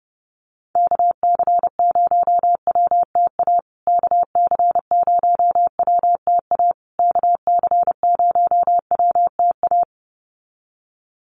Morse code, heard as a keyed tone, about 30 wpm, 700 Hz. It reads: KC0WTA KC0WTA KC0WTA
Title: USA Call Signs Repeated 3x 30wpm